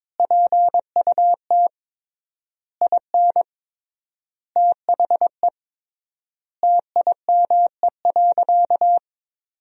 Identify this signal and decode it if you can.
Morse code, heard as a keyed tone, about 22 wpm, 700 Hz. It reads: PUT IN THE TIME.
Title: 3x Word-Spacing - Sentences from Top 200 Words - 22wpm